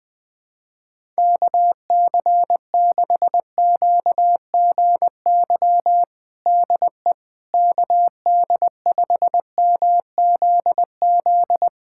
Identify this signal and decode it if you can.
Morse code, heard as a keyed tone, about 20 wpm, 700 Hz. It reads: KC6QGY DE KD5MZZ
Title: USA Callsign DE Callsign - 6 Characters - 20wpm